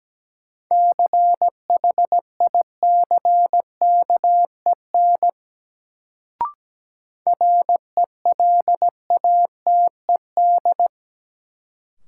Morse code, heard as a keyed tone, about 17 wpm, 700 Hz. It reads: CHICKEN RELATED
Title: Sentences from Top 1000 Words - Review Part 5 - New Word Drill - 17wpm